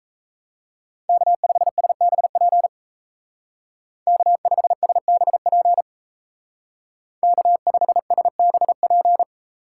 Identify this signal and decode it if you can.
Morse code, heard as a keyed tone, about 34 wpm, 700 Hz. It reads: K5SBP K5SBP K5SBP
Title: Speed Racing - USA Call Signs 28wpm